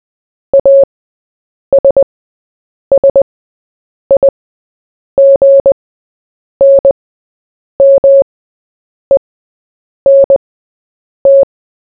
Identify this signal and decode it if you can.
Morse code, heard as a keyed tone, about 20 wpm, spaced as for 8 wpm, 550 Hz. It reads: ASSIGNMENT
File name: Short-Story-4-8wpm